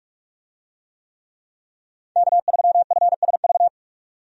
Morse code, heard as a keyed tone, about 46 wpm, 700 Hz. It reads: K3FSV
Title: USA Call Signs - 5 Characters - 45wpm